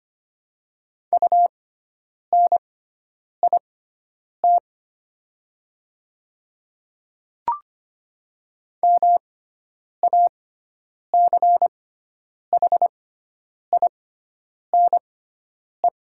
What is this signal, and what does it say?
Morse code, heard as a keyed tone, about 25 wpm, spaced as for 9 wpm, 700 Hz. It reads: UNIT MACHINE
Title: 6x Character Spacing - Top 500 Words - 25wpm